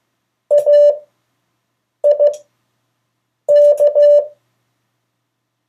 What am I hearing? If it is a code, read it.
Morse code, heard as a keyed tone, about 15 wpm, 580 Hz. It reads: A I K
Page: CW Class Review – Delaware Valley Radio Association